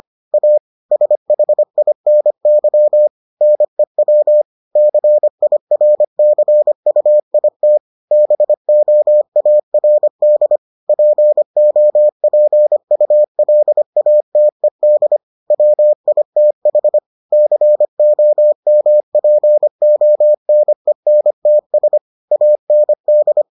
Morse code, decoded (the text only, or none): A SHINY NEW CIRCUIT BOARD POPULATED WITH COMPONENTS AND